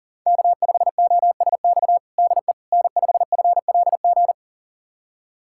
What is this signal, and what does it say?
Morse code, heard as a keyed tone, about 37 wpm, 690 Hz. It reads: K5OSX DE N5FLG
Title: USA Callsign DE Callsign - 5 Characters - 40wpm